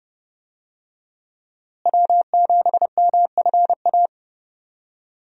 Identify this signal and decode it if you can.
Morse code, heard as a keyed tone, about 30 wpm, 700 Hz. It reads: W7MFA
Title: USA Call Signs - 5 Characters - Repeated 2x - 30wpm